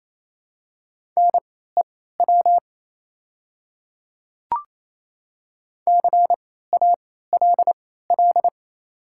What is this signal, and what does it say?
Morse code, heard as a keyed tone, about 28 wpm, spaced as for 16 wpm, 700 Hz. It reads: NEW CALL
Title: 3x Character Spacing - Top 500 Words - 28wpm